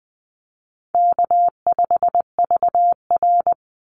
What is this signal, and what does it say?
Morse code, heard as a keyed tone, about 20 wpm, 700 Hz. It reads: K5VR
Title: Easy Call Signs Repeated 3x - 20wpm